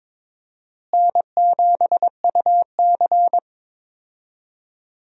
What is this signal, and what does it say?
Morse code, heard as a keyed tone, about 22 wpm, 700 Hz. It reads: N7UC
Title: USA Call Signs - 4 Characters - Repeated 2x - 22wpm